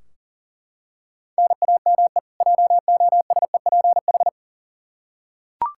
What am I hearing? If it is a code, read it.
Morse code, heard as a keyed tone, about 40 wpm, 700 Hz. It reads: NAME JOSEPH
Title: K1USN SST QSO Practice 40wpm